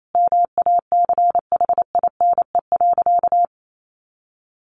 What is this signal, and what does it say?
Morse code, heard as a keyed tone, about 28 wpm, 700 Hz. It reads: MACHINE.
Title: A Sound Of Thunder 28wpm